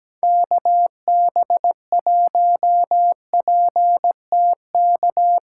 Morse code, heard as a keyed tone, about 18 wpm, 700 Hz. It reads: KB1PTK
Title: Speed Racing - USA 2x3 Call Signs 17wpm